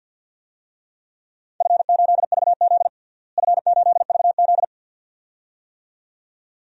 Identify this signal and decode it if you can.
Morse code, heard as a keyed tone, about 53 wpm, 700 Hz. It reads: F8VZ F8VZ
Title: DXCC-100 Call Signs - 4 Characters - Repeated 2x - 50wpm